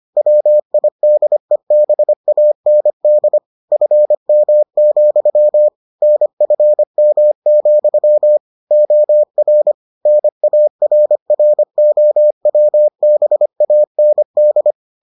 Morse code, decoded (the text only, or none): WIDEBAND FM, NFM, OR NARROWBAND